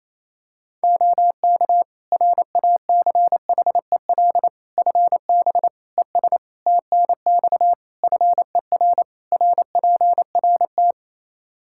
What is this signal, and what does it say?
Morse code, heard as a keyed tone, about 27 wpm, 700 Hz. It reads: OK RACHEL FB ES TNX FER RPRT
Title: Practice QSO - Exchange 2 - Info - 28wpm